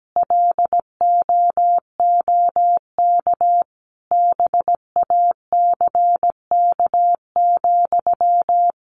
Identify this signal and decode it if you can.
Morse code, heard as a keyed tone, about 17 wpm, 700 Hz. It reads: LOOK BACK,
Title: Sentences from Top 800 Words 17wpm